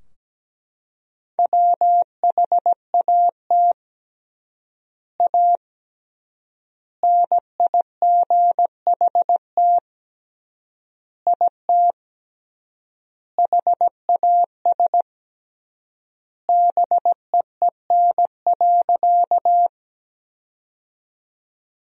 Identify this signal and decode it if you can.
Morse code, heard as a keyed tone, about 17 wpm, 700 Hz. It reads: WHAT A NIGHT IT HAS BEEN.
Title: 3x Word-Spacing - Sentences from Top 300 Words - 17wpm